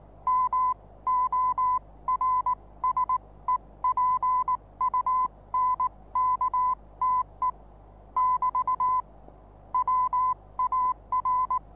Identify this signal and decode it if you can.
Morse code, heard as a keyed tone, about 19 wpm, spaced as for 16 wpm, 1010 Hz. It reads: MORSEPUNKTE <BT> WAR
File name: repos / Morsekurs / DJ4UF Morsen26